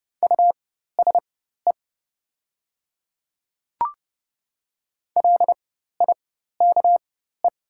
Morse code, heard as a keyed tone, about 30 wpm, spaced as for 14 wpm, 700 Hz. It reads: USE LIKE
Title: 4x Character Spacing - Top 100 Words - 30wpm